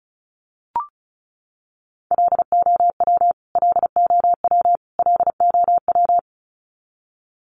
Morse code, heard as a keyed tone, about 34 wpm, 700 Hz. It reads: LOW LOW LOW
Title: Top 200 Words Repeated 35wpm